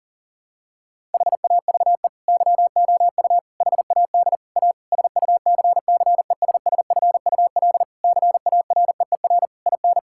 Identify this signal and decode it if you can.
Morse code, heard as a keyed tone, about 37 wpm, 700 Hz. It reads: HAVE YOU HAD A SUCCESSFUL CAREER IN
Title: Sentences from Top 1000 Words - Review Part 6 - 40wpm